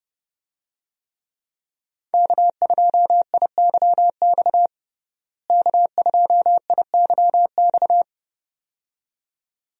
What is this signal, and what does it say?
Morse code, heard as a keyed tone, about 30 wpm, 700 Hz. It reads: K2IYX K2IYX
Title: USA Call Signs - 5 Characters - Repeated 2x - 30wpm